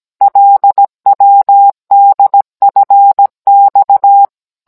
Morse code, heard as a keyed tone, about 17 wpm, 800 Hz. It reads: LWDFX